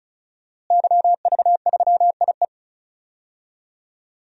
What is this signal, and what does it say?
Morse code, heard as a keyed tone, about 34 wpm, 700 Hz. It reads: YV3IE
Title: DXCC-100 Call Signs - 5 Characters - Repeated 2x - 35wpm